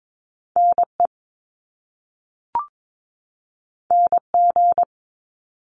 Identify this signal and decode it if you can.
Morse code, heard as a keyed tone, about 22 wpm, 700 Hz. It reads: NE NG
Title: Top Two Letter Combinations in Words 22wpm